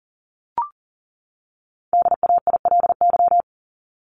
Morse code, heard as a keyed tone, about 40 wpm, 700 Hz. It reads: DAILY
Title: LY Suffix - Top 100 Words with Base Words 40wpm